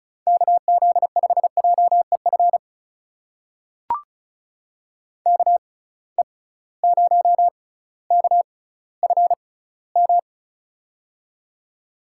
Morse code, heard as a keyed tone, about 36 wpm, spaced as for 18 wpm, 700 Hz. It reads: KZ5JEF KE0KFM
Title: USA Call Signs - 6 Characters - 6x Character Spacing - 35wpm